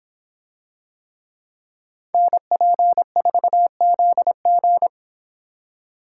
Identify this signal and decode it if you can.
Morse code, heard as a keyed tone, about 24 wpm, 700 Hz. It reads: NP4ZG